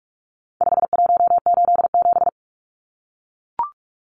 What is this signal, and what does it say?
Morse code, heard as a keyed tone, about 44 wpm, 700 Hz. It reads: 5187
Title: Numbers - 4-Digit 45wpm